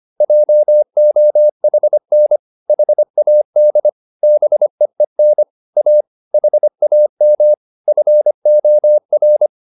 Morse code, heard as a keyed tone, about 25 wpm, 600 Hz. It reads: JOHN HAD BEEN A HAM FOR